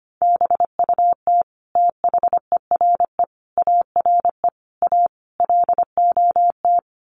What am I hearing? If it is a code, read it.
Morse code, heard as a keyed tone, about 25 wpm, 700 Hz. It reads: BUT THERE ARE A LOT
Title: Sets of 5 Words Encore2 25wpm